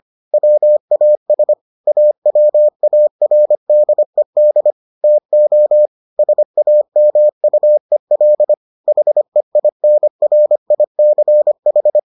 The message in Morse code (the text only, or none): WAS AWARDED TO SAMUEL HEINRICH